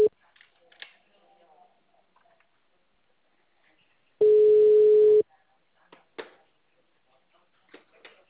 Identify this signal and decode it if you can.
Ringback tone (Europe, 425 Hz), with speech.